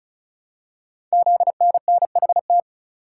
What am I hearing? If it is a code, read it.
Morse code, heard as a keyed tone, about 36 wpm, 700 Hz. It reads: ZNNHT
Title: Code Groups - 5 Letters - 35wpm